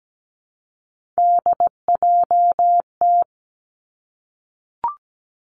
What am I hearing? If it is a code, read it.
Morse code, heard as a keyed tone, about 17 wpm, 700 Hz. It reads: DJT
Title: Three Challenging Letter-Number Combinations 17wpm